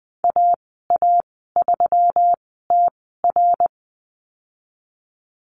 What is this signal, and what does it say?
Morse code, heard as a keyed tone, about 19 wpm, spaced as for 14 wpm, 700 Hz. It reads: AA3TR
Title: 179 Lesson 29g - Slash - Call Signs 20wpm